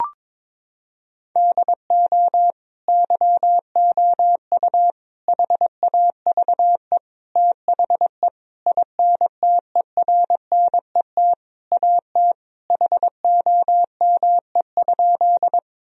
Morse code, heard as a keyed tone, about 22 wpm, 700 Hz. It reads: DO YOU HAVE THE INTERNET AT HOME?
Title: Sentences from Top 1000 Words - Review Part 4 - 22wpm